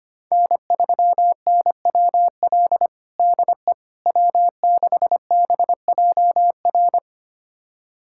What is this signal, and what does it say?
Morse code, heard as a keyed tone, about 25 wpm, 700 Hz. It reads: N3NWL DE W6BJR
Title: USA Callsign DE Callsign - 5 Characters - 25wpm